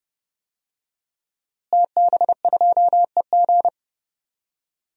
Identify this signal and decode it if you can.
Morse code, heard as a keyed tone, about 32 wpm, 700 Hz. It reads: TB2EG
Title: DXCC-100 Call Signs - 5 Characters - Repeated 3x - 30wpm